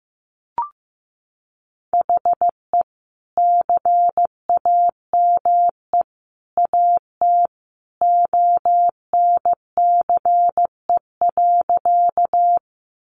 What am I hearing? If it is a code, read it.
Morse code, heard as a keyed tone, about 15 wpm, 700 Hz. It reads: HE CAME AT ONCE.
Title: Sentences from Top 300 Words 15wpm